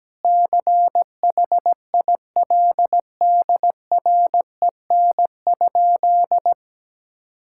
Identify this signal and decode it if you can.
Morse code, heard as a keyed tone, about 17 wpm, 700 Hz. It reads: CHILDREN?
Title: Sentences from Top 1000 Words - Review Part 6 - 17wpm